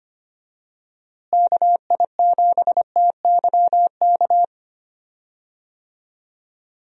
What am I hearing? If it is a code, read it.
Morse code, heard as a keyed tone, about 25 wpm, 700 Hz. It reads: KI7TYK
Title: USA Call Signs - 6 Characters - Repeated 2x - 25wpm